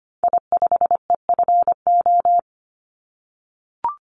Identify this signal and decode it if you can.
Morse code, heard as a keyed tone, about 25 wpm, 700 Hz. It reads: I5EFO
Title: Call Signs - 25wpm